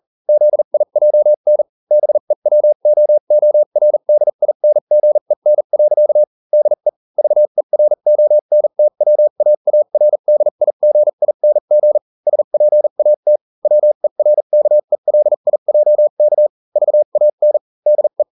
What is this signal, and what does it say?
Morse code, heard as a keyed tone, about 40 wpm, spaced as for 35 wpm, 600 Hz. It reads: ZIJN BEWOORDINGEN. DE VERONTWAARDIGING SPAT WERKELIJK VAN DE